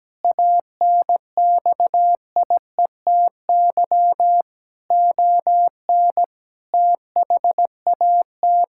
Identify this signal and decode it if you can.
Morse code, heard as a keyed tone, about 17 wpm, 700 Hz. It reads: ANXIETY ON THAT
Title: Sentences from Top 1600 Words 17wpm